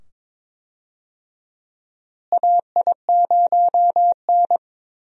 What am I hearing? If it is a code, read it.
Morse code, heard as a keyed tone, about 22 wpm, 700 Hz. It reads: AI0N
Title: USA Call Signs - 4 Characters - Repeated 2x - 22wpm